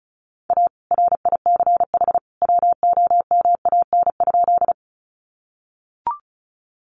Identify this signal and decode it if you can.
Morse code, heard as a keyed tone, about 34 wpm, 700 Hz. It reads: A RICH WOMAN?
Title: Sentences from Top 1100 Words 35wpm